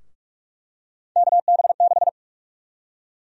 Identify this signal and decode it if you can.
Morse code, heard as a keyed tone, about 47 wpm, 700 Hz. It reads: KB6
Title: USA Call Sign Prefixes - Repeated 2x - 45wpm